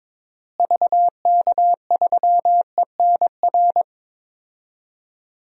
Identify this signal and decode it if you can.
Morse code, heard as a keyed tone, about 22 wpm, 700 Hz. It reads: VK3ENR
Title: DXCC-100 Call Signs - 6 Characters - Repeated 3x - 22wpm